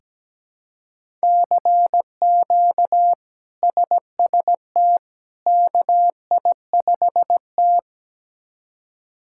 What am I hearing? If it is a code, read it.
Morse code, heard as a keyed tone, about 17 wpm, 700 Hz. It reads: CQ SST KI5T
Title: K1USN SST QSO Practice - 17wpm